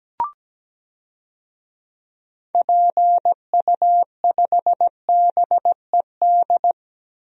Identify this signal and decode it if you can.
Morse code, heard as a keyed tone, about 17 wpm, 700 Hz. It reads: PU5BED
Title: DXCC-100 Call Signs - 6 Characters - 17wpm